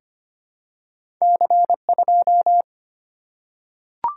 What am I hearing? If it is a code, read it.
Morse code, heard as a keyed tone, about 25 wpm, 700 Hz. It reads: C2
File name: Two Letter-Number Combinations - 25wpm